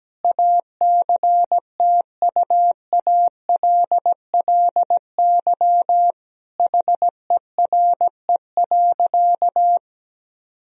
Morse code, heard as a keyed tone, about 17 wpm, 700 Hz. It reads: ACTUALLY HERE.
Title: Sentences from Top 1000 Words - Review Part 4 - 17wpm